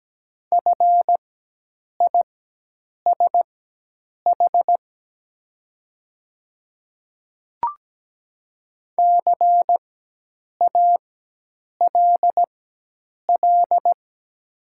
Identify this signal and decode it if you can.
Morse code, heard as a keyed tone, about 17 wpm, spaced as for 8 wpm, 700 Hz. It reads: FISH CALL
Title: 4x Character Spacing - Top 500 Words - 17wpm